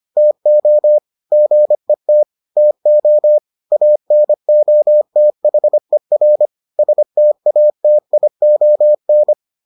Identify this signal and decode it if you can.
Morse code, heard as a keyed tone, about 25 wpm, 600 Hz. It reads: TO GET TO ANOTHER STATION